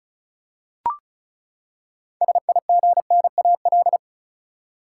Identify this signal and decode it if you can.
Morse code, heard as a keyed tone, about 34 wpm, 700 Hz. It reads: SIGNAL
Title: Six Letter Words - 35wpm